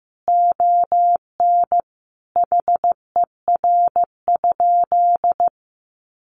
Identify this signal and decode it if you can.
Morse code, heard as a keyed tone, about 15 wpm, 700 Hz. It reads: ON HER?
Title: Sentences from Top 800 Words 15wpm